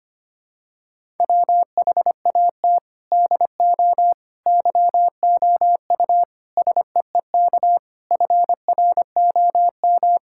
Morse code, heard as a keyed tone, about 25 wpm, 700 Hz. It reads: WHAT DO YOU SEEK FROM
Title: Sentences from Top 1600 Words 25wpm